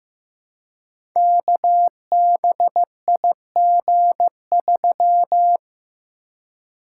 Morse code, heard as a keyed tone, about 15 wpm, 700 Hz. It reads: KBIG3
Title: Code Groups - 5 Letter:Numbers - 15wpm